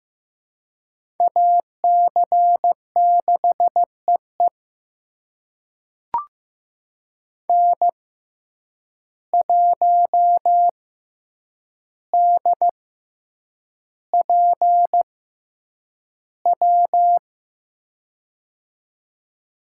Morse code, heard as a keyed tone, about 15 wpm, spaced as for 8 wpm, 700 Hz. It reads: AC6EE N1DPW
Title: USA Call Signs - 5 Characters - 6x Character Spacing - 15wpm